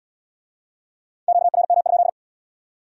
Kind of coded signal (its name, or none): Morse code